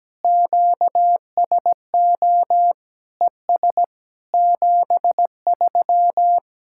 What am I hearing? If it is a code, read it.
Morse code, heard as a keyed tone, about 17 wpm, 700 Hz. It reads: QSO ES 73
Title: Speed-Racing Common QSO Phrases - 17wpm